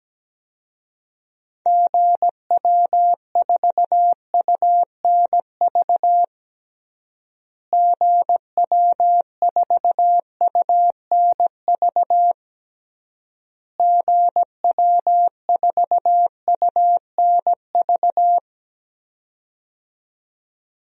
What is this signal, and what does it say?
Morse code, heard as a keyed tone, about 17 wpm, 700 Hz. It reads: GW4UNV GW4UNV GW4UNV
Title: DXCC-100 Call Signs - 6 Characters - Repeated 3x - 17wpm